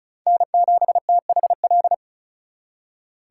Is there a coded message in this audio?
Yes: Morse code from 0.3 to 1.9 s, heard as a keyed tone, about 44 wpm, 700 Hz. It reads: N7THL